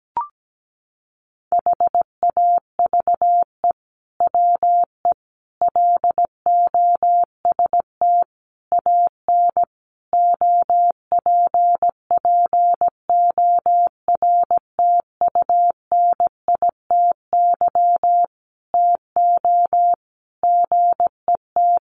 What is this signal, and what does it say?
Morse code, heard as a keyed tone, about 17 wpm, 700 Hz. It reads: HAVE WE LOST AN OPPORTUNITY TO GET
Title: Sentences from Top 1400 Words 17wpm